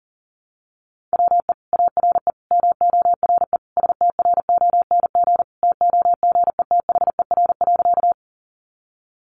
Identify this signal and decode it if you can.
Morse code, heard as a keyed tone, about 40 wpm, 700 Hz. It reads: WE ARE MORE STRONG TOGETHER.
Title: Sentences from Top 500 Words 40wpm